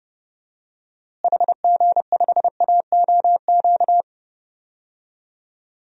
Morse code, heard as a keyed tone, about 32 wpm, 700 Hz. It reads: HG5AOQ
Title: DXCC-100 Call Signs - 6 Characters - Repeated 3x - 30wpm